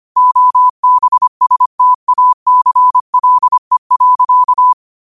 Morse code, heard as a keyed tone, about 25 wpm, 1000 Hz. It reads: OBSTACLE.